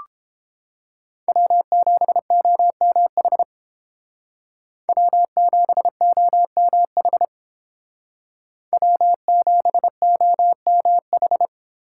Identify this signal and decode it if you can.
Morse code, heard as a keyed tone, about 29 wpm, 700 Hz. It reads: W7OMH W7OMH W7OMH